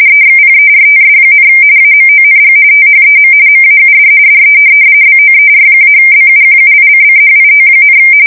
RTTY, about 45.45 baud. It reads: EXPERTS, AMONG THEM MAUD DE BOER-BUQUICCHIO,
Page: 170 Shift 2125/2295